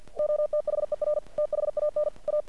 Morse code, heard as a keyed tone, about 50 wpm, 600 Hz. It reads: OTHER THAN A
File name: archived / CRSnet